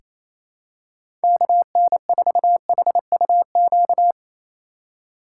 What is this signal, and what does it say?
Morse code, heard as a keyed tone, about 28 wpm, 700 Hz. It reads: KN4HUQ
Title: USA Call Signs - 6x Character Spacing - 28wpm